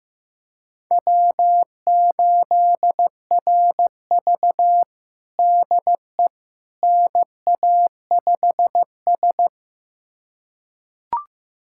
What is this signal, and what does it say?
Morse code, heard as a keyed tone, about 15 wpm, 700 Hz. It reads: W8RV DE NA5S
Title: USA Callsign DE Callsign - 4 Characters - 15wpm